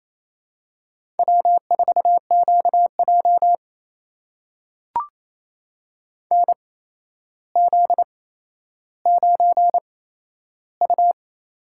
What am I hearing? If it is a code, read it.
Morse code, heard as a keyed tone, about 28 wpm, spaced as for 12 wpm, 700 Hz. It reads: W4QJ NZ9U
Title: USA Call Signs - 8x Character Spacing - 28wpm